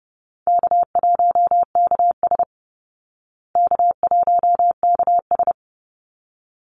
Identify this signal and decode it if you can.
Morse code, heard as a keyed tone, about 30 wpm, 700 Hz. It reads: K1KS K1KS
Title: Easy Call Signs Repeated 2x - 30wpm